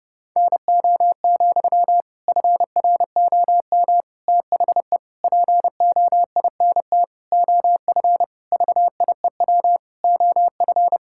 Morse code, heard as a keyed tone, about 29 wpm, 700 Hz. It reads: NO, FROM THE POINT OF VIEW OF
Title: Sentences from Top 1000 Words - Review Part 6 - 30wpm